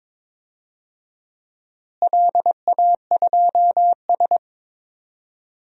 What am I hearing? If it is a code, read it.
Morse code, heard as a keyed tone, about 22 wpm, 700 Hz. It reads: LA2S
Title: DXCC-100 Call Signs - 4 Characters - Repeated 3x - 22wpm